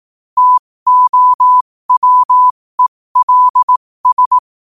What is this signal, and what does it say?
Morse code, heard as a keyed tone, about 18 wpm, spaced as for 15 wpm, 1000 Hz. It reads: TOWELS
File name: qotdfiles / 15wpm